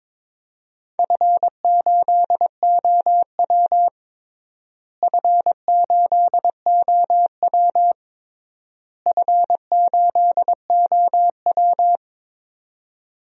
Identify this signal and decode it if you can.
Morse code, heard as a keyed tone, about 22 wpm, 700 Hz. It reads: F8OW F8OW F8OW
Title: DXCC-100 Call Signs - 4 Characters - Repeated 3x - 22wpm